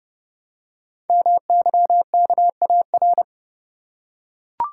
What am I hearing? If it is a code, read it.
Morse code, heard as a keyed tone, about 28 wpm, 700 Hz. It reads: MYKAR
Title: Code Groups - 5 Letters - 30wpm